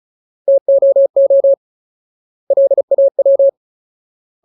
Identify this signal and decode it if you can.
Morse code, heard as a keyed tone, about 34 wpm, 550 Hz. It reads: TOO LAW
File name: Ultimate Word List / 35WPM Ultimate